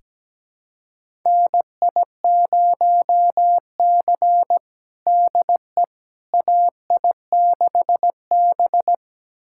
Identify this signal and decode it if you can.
Morse code, heard as a keyed tone, about 17 wpm, 700 Hz. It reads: NI0C DE AI6B
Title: USA Callsign DE Callsign - 4 Characters - 17wpm